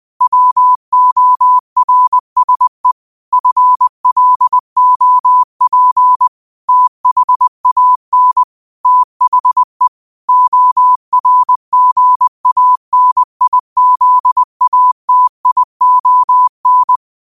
Morse code, heard as a keyed tone, about 20 wpm, 1000 Hz. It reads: WORSE FLOP THAN THE ORGANIZATION
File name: qotdfiles / 20wpm